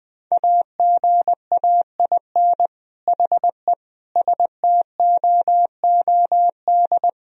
Morse code, heard as a keyed tone, about 20 wpm, 700 Hz. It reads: AGAIN HE STOOD
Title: Sentences from Top 1000 Words - Review Part 3 - 20wpm